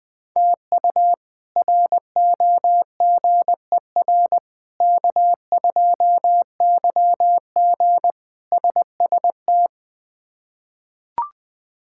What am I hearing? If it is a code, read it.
Morse code, heard as a keyed tone, about 20 wpm, 700 Hz. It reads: TU ROGER K2YG SST
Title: K1USN SST QSO Practice - 20wpm